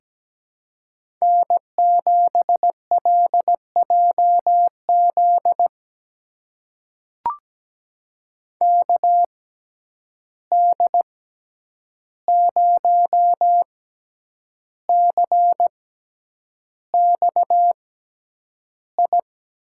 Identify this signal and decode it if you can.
Morse code, heard as a keyed tone, about 17 wpm, spaced as for 8 wpm, 700 Hz. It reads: N7LJZ KD0CXI
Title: USA Call Signs - 6x Character Spacing - 17wpm